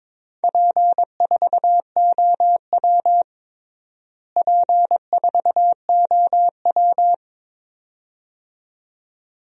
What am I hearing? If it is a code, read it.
Morse code, heard as a keyed tone, about 22 wpm, 700 Hz. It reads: P4OW P4OW
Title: DXCC-100 Call Signs - 4 Characters - Repeated 2x - 22wpm